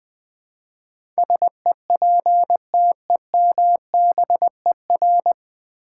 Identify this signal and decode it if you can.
Morse code, heard as a keyed tone, about 20 wpm, 700 Hz. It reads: SEPTEMBER
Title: Nine Letter Words - 20wpm